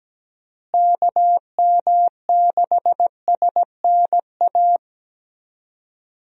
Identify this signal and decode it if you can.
Morse code, heard as a keyed tone, about 17 wpm, 700 Hz. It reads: KM6SNA
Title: USA Call Signs - 6 Characters - Repeated 3x - 17wpm